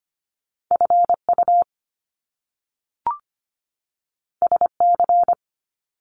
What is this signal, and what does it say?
Morse code, heard as a keyed tone, about 25 wpm, 700 Hz. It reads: FU SC
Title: Top Two Letter Combinations in Words 25wpm